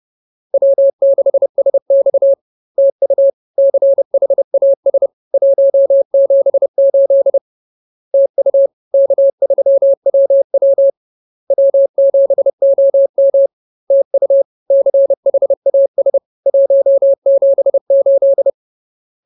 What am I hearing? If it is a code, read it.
Morse code, heard as a keyed tone, about 30 wpm, 550 Hz. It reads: W6SX TU CHAS 178 TU K3WW W7OM TU CHAS 178
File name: CWT-Chapter-04